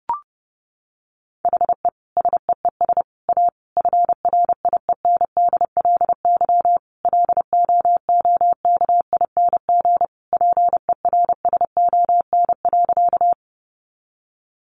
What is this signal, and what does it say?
Morse code, heard as a keyed tone, about 30 wpm, 700 Hz. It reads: HE SEES A FRIENDLY LOOKING PERSON.
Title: Easy Phrases 30wpm